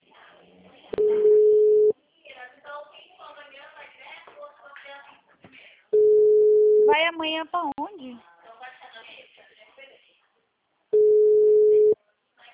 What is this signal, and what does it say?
Ringback tone (Europe, 425 Hz), with speech.
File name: out- -2046- - - .75088